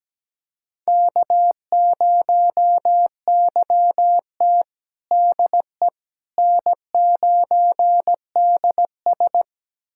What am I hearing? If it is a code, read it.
Morse code, heard as a keyed tone, about 17 wpm, 700 Hz. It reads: K0YT DE N9DS
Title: USA Callsign DE Callsign - 4 Characters - 17wpm